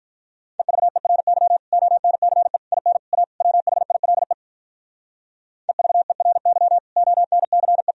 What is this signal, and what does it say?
Morse code, heard as a keyed tone, about 45 wpm, 700 Hz. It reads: EVERY ONCE IN A WHILE EVERY ONCE